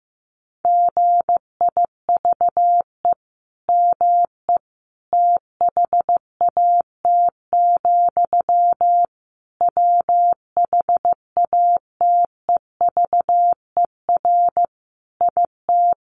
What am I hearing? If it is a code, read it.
Morse code, heard as a keyed tone, about 15 wpm, 700 Hz. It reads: GIVE ME THAT, WHATEVER IT
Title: Sentences from Top 1100 Words 15wpm